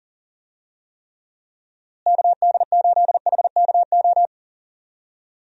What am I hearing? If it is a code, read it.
Morse code, heard as a keyed tone, about 40 wpm, 700 Hz. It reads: KD8HKO
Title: USA Call Signs - 6 Characters - Repeated 3x - 40wpm